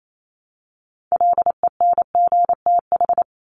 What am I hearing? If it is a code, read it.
Morse code, heard as a keyed tone, about 28 wpm, 700 Hz. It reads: LENGTH
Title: Top 751-1000 Words 28wpm